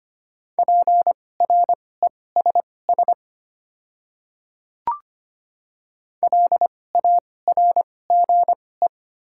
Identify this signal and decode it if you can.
Morse code, heard as a keyed tone, about 25 wpm, spaced as for 18 wpm, 700 Hz. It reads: PRESS LARGE
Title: 2x Character Spacing - Top 300 Words - 25wpm